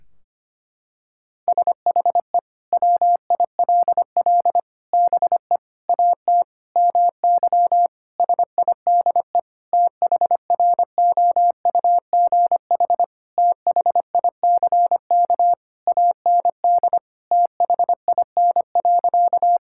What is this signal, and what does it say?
Morse code, heard as a keyed tone, about 25 wpm, 700 Hz. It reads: SHE WILL BE AT MY SIDE THROUGH THICK AND THIN.
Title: Sentences from Top 1600 Words 25wpm